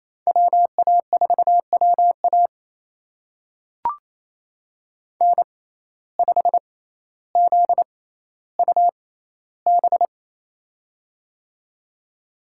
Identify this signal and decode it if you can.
Morse code, heard as a keyed tone, about 28 wpm, spaced as for 14 wpm, 700 Hz. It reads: WA4WA N5ZUB
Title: USA Call Signs - 6x Character Spacing - 28wpm